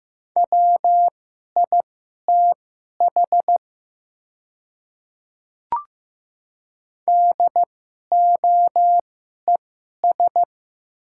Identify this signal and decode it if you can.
Morse code, heard as a keyed tone, about 15 wpm, spaced as for 11 wpm, 700 Hz. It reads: WITH DOES
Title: 2x Character Spacing - Top 200 Words - 15wpm